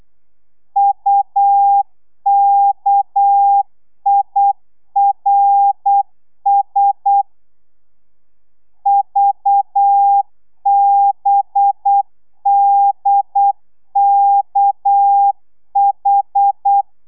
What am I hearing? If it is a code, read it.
Morse code, heard as a keyed tone, about 8 wpm, 800 Hz. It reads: UKIRS VBDKH
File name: telegrafia / i2tfj / Lez6 35